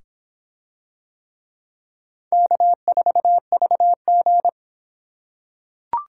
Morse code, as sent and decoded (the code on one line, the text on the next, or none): -.- ....- ...- --.
K4VG